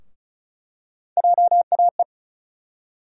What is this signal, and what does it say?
Morse code, heard as a keyed tone, about 36 wpm, 700 Hz. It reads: JAE
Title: USA Call Sign Postfixes - 3 Letters 35wpm